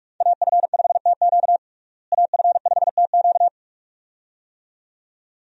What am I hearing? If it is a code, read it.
Morse code, heard as a keyed tone, about 41 wpm, 700 Hz. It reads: AF5TQ AF5TQ
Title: USA Call Signs - 5 Characters - Repeated 2x - 45wpm